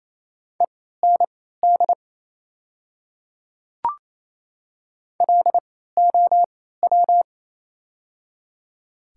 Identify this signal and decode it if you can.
Morse code, heard as a keyed tone, about 27 wpm, spaced as for 16 wpm, 690 Hz. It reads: END LOW
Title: 3x Character Spacing - Top 200 Words - 28wpm